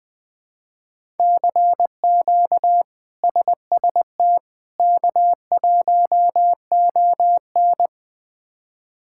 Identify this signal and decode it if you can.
Morse code, heard as a keyed tone, about 20 wpm, 700 Hz. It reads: CQ SST K1ON
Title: K1USN SST QSO Practice - 20wpm